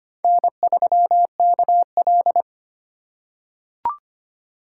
Morse code, heard as a keyed tone, about 25 wpm, 700 Hz. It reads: N3KL
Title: USA Call Signs - 8x Character Spacing - 25wpm